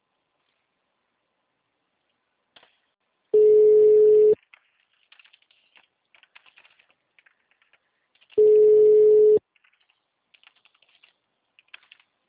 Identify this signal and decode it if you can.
Ringback tone (Europe, 425 Hz).
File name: out- -2020- - - .46294